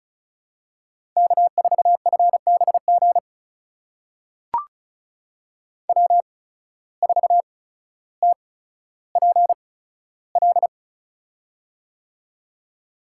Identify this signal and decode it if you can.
Morse code, heard as a keyed tone, about 36 wpm, spaced as for 15 wpm, 700 Hz. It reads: K4FBG W4TPL
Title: USA Call Signs - 8x Character Spacing - 35wpm